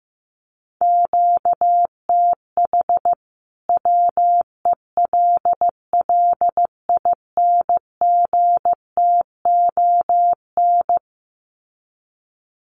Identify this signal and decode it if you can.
Morse code, heard as a keyed tone, about 15 wpm, 700 Hz. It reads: QTH WELLINGTON
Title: Common QSO Phrases 15wpm